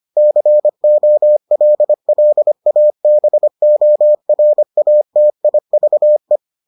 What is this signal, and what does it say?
Morse code, heard as a keyed tone, about 25 wpm, 600 Hz. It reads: COLLABORATIVE